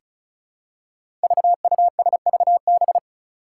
Morse code, heard as a keyed tone, about 34 wpm, 700 Hz. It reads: VUSVB
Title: Five Challenging Letter-Number Combinations 35wpm